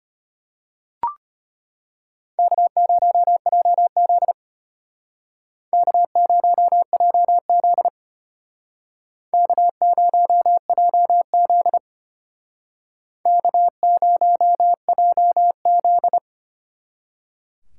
Morse code, heard as a keyed tone, about 32 wpm, 700 Hz. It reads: K0JZ K0JZ K0JZ K0JZ